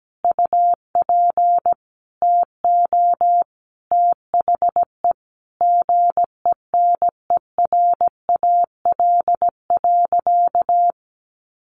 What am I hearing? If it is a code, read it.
Morse code, heard as a keyed tone, about 17 wpm, 700 Hz. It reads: UP TO THE GENERAL.
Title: Sentences from Top 1500 Words 17wpm